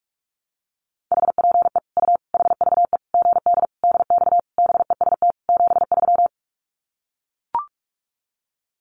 Morse code, heard as a keyed tone, about 44 wpm, 700 Hz. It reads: HPE U HVE GD DX BEST 73
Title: Common QSO Phrases 45wpm